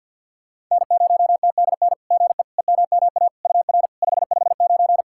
Morse code, heard as a keyed tone, about 52 wpm, 690 Hz. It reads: N0TDN GE EMMA UR 559
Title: SOTA QSO Practice 50wpm